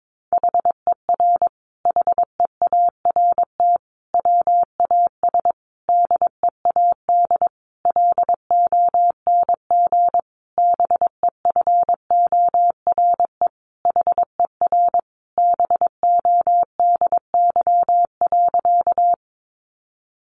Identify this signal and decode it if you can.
Morse code, heard as a keyed tone, about 22 wpm, 700 Hz. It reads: HER HEART WAS DEAD LONG BEFORE HER BODY.
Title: Sentences from Top 1500 Words 22wpm